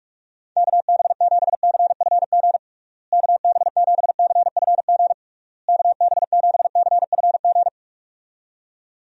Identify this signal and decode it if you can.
Morse code, heard as a keyed tone, about 43 wpm, 700 Hz. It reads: KB7CFG KB7CFG KB7CFG
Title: USA Call Signs - 6 Characters - Repeated 3x - 45wpm